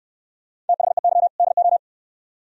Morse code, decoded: THEY DO

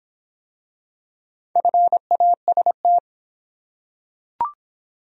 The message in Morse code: FAST

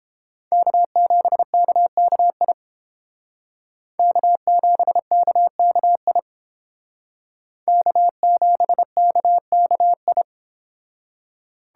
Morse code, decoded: K7KKI K7KKI K7KKI

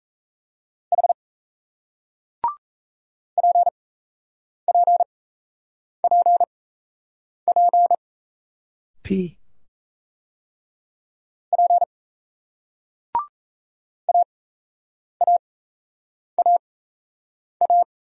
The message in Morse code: H PPPP P AAAA

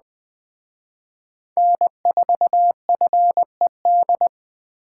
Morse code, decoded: N4FED

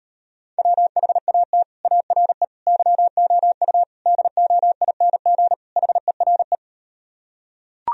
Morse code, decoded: WHAT ARE YOU DOING HERE

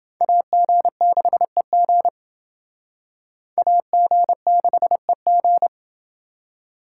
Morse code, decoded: AG6EG AG6EG

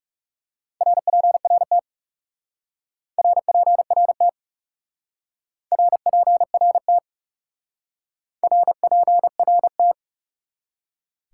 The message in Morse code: RPRT RPRT RPRT RPRT

